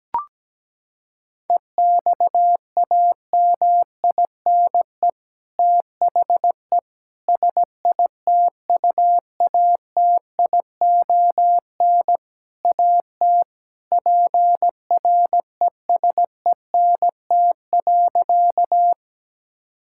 Morse code, heard as a keyed tone, about 17 wpm, 700 Hz. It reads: EXAMINE THE SITUATION AT PRESENT.
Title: Sentences from Top 1600 Words 17wpm